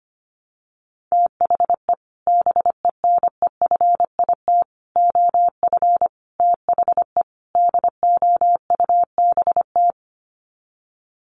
Morse code, heard as a keyed tone, about 25 wpm, 700 Hz. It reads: THE BENEFIT OF THE DOUBT
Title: Sets of 5 Words Encore 25wpm